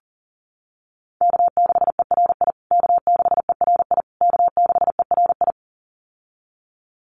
Morse code, heard as a keyed tone, about 37 wpm, 700 Hz. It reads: K6ERI K6ERI K6ERI
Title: USA Call Signs Repeated 3x 40wpm